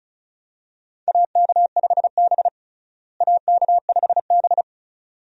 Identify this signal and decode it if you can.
Morse code, heard as a keyed tone, about 36 wpm, 700 Hz. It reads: AK5B AK5B
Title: USA Call Signs - 4 Characters - Repeated 2x - 35wpm